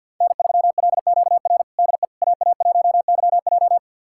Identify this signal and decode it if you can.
Morse code, heard as a keyed tone, about 51 wpm, 700 Hz. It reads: N3LQR DE AA1YJ
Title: USA Callsign DE Callsign - 5 Characters - 50wpm